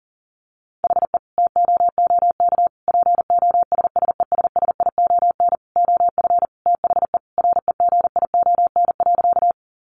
Morse code, heard as a keyed tone, about 40 wpm, 700 Hz. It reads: HE TOOK POSSESSION OF THE REGION.
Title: Sentences from Top 1500 Words 40wpm